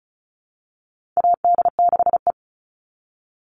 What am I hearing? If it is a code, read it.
Morse code, heard as a keyed tone, about 34 wpm, 700 Hz. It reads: AD6E
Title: Call Signs Repeated 2x - 35wpm